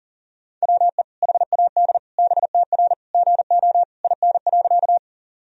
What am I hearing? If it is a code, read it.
Morse code, heard as a keyed tone, about 40 wpm, 700 Hz. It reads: WE HAD BTR GO IN.
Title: Sentences from Top 400 Words and Abbreviations 40wpm